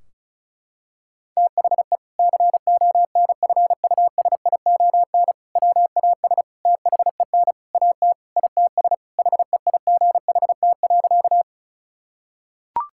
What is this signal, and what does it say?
Morse code, heard as a keyed tone, about 36 wpm, 700 Hz. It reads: THE CONFUSION WAS THEN AT ITS HEIGHT.
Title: Sentences from Top 1600 Words 35wpm